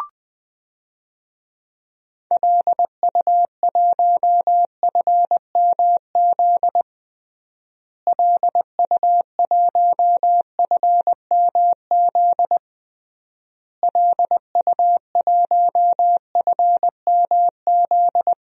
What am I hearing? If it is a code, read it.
Morse code, heard as a keyed tone, about 20 wpm, 700 Hz. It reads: LU1FMZ LU1FMZ LU1FMZ
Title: DXCC-100 Call Signs - 6 Characters - Repeated 3x - 20wpm